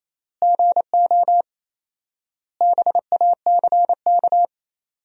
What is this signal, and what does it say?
Morse code, heard as a keyed tone, about 27 wpm, 700 Hz. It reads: GO BACK
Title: 4x Word-Spacing - Sentences from Top 500 Words - 28wpm